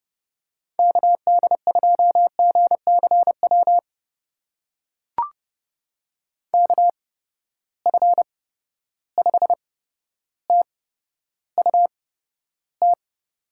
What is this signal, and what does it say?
Morse code, heard as a keyed tone, about 30 wpm, spaced as for 13 wpm, 700 Hz. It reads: KD2GCW KF5TUT
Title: USA Call Signs - 6 Characters - 8x Character Spacing - 30wpm